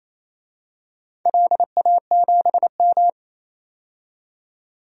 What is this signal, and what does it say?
Morse code, heard as a keyed tone, about 28 wpm, 700 Hz. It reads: LA7M
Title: DXCC-100 Call Signs - 4 Characters - 28wpm